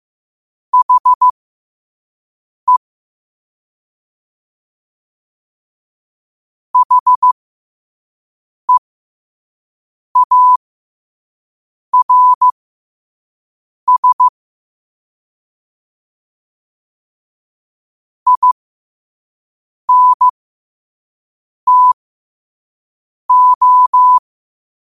Morse code, heard as a keyed tone, about 15 wpm, spaced as for 5 wpm, 1000 Hz. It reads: HE HEARS INTO
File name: qotdfiles / 5wpm